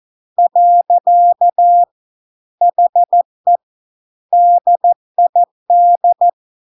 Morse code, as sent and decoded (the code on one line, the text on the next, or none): .-.-.- / .... / -.. .. -..
. HE DID